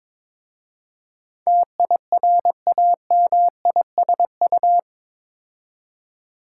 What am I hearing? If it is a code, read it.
Morse code, heard as a keyed tone, about 22 wpm, 690 Hz. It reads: TIRAMISU
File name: Difficult Words with Hints - 22wpm